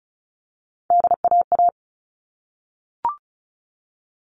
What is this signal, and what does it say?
Morse code, heard as a keyed tone, about 34 wpm, 700 Hz. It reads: DAA
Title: All Cut Numbers - 3-Digit 35wpm